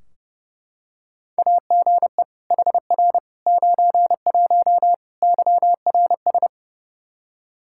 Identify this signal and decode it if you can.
Morse code, heard as a keyed tone, about 30 wpm, 700 Hz. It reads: AGE HR 91 YRS
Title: Practice QSO - Exchange 3 - More Info - 30wpm